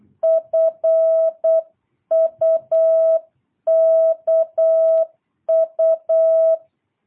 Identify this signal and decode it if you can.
Morse code, heard as a keyed tone, about 8 wpm, 660 Hz. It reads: FUKU
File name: jh8cbh / 2022cw